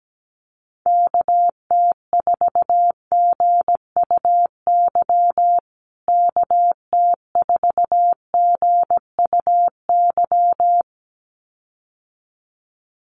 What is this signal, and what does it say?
Morse code, heard as a keyed tone, about 17 wpm, 700 Hz. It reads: KT4GUY KT4GUY
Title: USA Call Signs Repeated 2x 17wpm